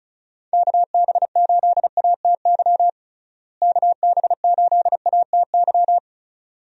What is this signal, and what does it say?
Morse code, heard as a keyed tone, about 34 wpm, 700 Hz. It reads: KB8ATY KB8ATY
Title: USA Call Signs - 6 Characters - Repeated 2x - 35wpm